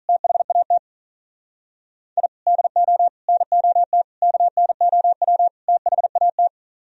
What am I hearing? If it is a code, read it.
Morse code, heard as a keyed tone, about 45 wpm, 700 Hz. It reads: THAT I DO NOT KNOW THAT